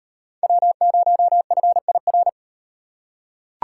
Morse code, heard as a keyed tone, about 34 wpm, 700 Hz. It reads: W0FIR